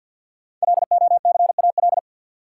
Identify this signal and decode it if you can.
Morse code, heard as a keyed tone, about 46 wpm, 700 Hz. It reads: LOCAL
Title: Sentences from Top 1000 Words - Review Part 3 - New Word Drill - 50wpm